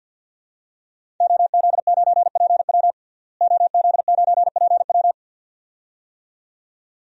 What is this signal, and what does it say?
Morse code, heard as a keyed tone, about 52 wpm, 700 Hz. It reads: OZ9PW OZ9PW
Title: DXCC-100 Call Signs - 5 Characters - Repeated 2x - 50wpm